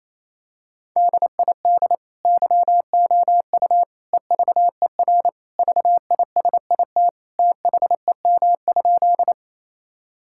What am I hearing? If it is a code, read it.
Morse code, heard as a keyed tone, about 28 wpm, 700 Hz. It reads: DID YOU EVER VISIT THEM?
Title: Sentences from Top 1000 Words - Review Part 6 - 28wpm